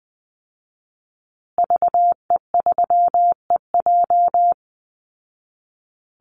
Morse code, heard as a keyed tone, about 20 wpm, 700 Hz. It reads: VE3EJ
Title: Call Signs Repeated 2x - 20wpm